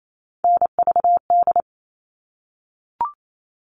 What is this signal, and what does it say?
Morse code, heard as a keyed tone, about 28 wpm, 700 Hz. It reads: NVD
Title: All Cut Numbers - 3-Digit 28wpm